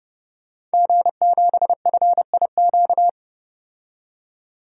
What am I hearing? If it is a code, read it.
Morse code, heard as a keyed tone, about 30 wpm, 700 Hz. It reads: G7FIQ
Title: DXCC-100 Call Signs - 5 Characters - 30wpm